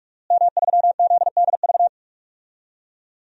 Morse code, heard as a keyed tone, about 42 wpm, 700 Hz. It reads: M3ZDV
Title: DXCC-100 Call Signs - 5 Characters - Repeated 3x - 45wpm